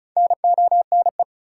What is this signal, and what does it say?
Morse code, heard as a keyed tone, about 36 wpm, 700 Hz. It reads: NONE
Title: Four Letter Words - 35wpm